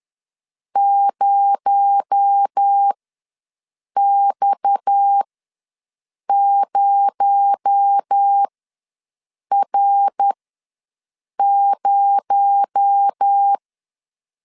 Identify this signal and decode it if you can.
Morse code, heard as a keyed tone, about 10 wpm, spaced as for 6 wpm, 800 Hz. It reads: 0X0R0
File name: LEZIONI / 6 PKX0 / 6 PKX0 ws 10wpm 4220